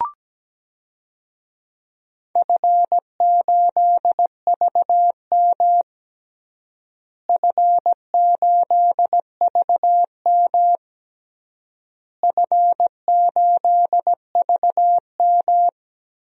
Morse code, heard as a keyed tone, about 17 wpm, 700 Hz. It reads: F8VM F8VM F8VM
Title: DXCC-100 Call Signs - 4 Characters - Repeated 3x - 17wpm